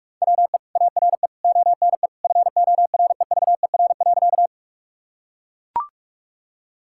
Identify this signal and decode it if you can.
Morse code, heard as a keyed tone, about 41 wpm, 700 Hz. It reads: WE ARE ONE FOREVER.
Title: Sentences from Top 1600 Words 45wpm